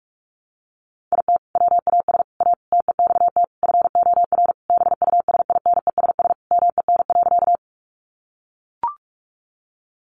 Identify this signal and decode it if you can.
Morse code, heard as a keyed tone, about 44 wpm, 700 Hz. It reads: IT WAS A TEXT FOR BUSINESS MEN.
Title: Sentences from Top 1200 Words 45wpm